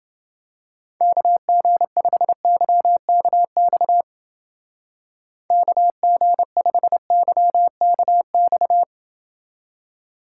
Morse code, heard as a keyed tone, about 29 wpm, 700 Hz. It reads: KG5YKX KG5YKX